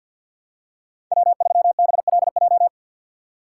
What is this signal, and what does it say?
Morse code, heard as a keyed tone, about 53 wpm, 700 Hz. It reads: W3BLJ
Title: USA Call Signs - 5 Characters - 6x Character Spacing - 50wpm